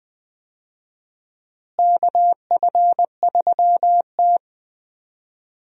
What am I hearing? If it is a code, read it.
Morse code, heard as a keyed tone, about 20 wpm, 700 Hz. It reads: KF3T
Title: USA Call Signs - 4 Characters - Repeated 3x - 20wpm